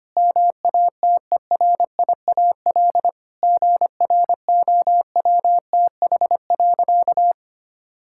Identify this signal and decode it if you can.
Morse code, heard as a keyed tone, about 25 wpm, 700 Hz. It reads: MATERIAL GROWTH.
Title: Sentences from Top 1000 Words - Review Part 5 - 25wpm